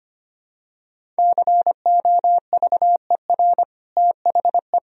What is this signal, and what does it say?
Morse code, heard as a keyed tone, about 25 wpm, 700 Hz. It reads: COVER THE
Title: Sentences from Top 1000 Words - Review Part 1 - 25wpm